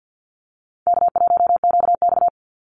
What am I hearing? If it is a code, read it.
Morse code, heard as a keyed tone, about 49 wpm, 700 Hz. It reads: K1QX
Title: Call Signs Repeated 2x - 50wpm